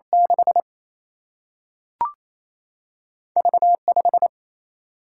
Morse code, heard as a keyed tone, about 28 wpm, 700 Hz. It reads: 6 V 5
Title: Single-Letter Single-Number 28wpm